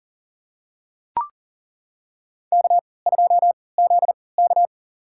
Morse code, heard as a keyed tone, about 40 wpm, spaced as for 23 wpm, 700 Hz. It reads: K2ZK
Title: USA Call Signs - 4 Characters - 3x Character Spacing - 40wpm